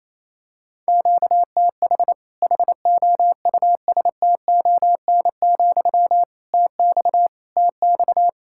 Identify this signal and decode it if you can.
Morse code, heard as a keyed tone, about 28 wpm, 700 Hz. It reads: QTH HOUSTON, TX TX
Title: Practice QSO - Exchange 1 - Beginning - 28wpm